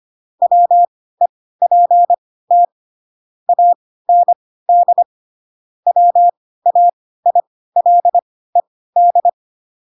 Morse code, heard as a keyed tone, about 25 wpm, spaced as for 16 wpm, 700 Hz. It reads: WEPT AND WAILED
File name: Pinocchio / Pinocchio ch3 10 25 ik1hge